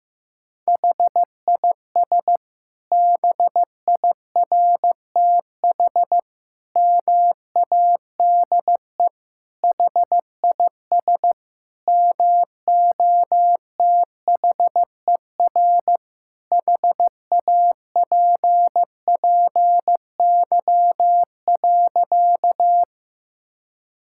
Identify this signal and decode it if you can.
Morse code, heard as a keyed tone, about 15 wpm, 700 Hz. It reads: HIS BIRTH MADE HIS MOTHER HAPPY.
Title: Sentences from Top 1000 Words - Review Part 5 - 15wpm